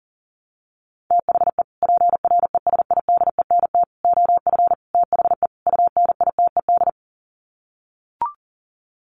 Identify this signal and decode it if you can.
Morse code, heard as a keyed tone, about 40 wpm, 700 Hz. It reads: THE PRESIDENT OF THE UNITED
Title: Sets of 5 Words 40wpm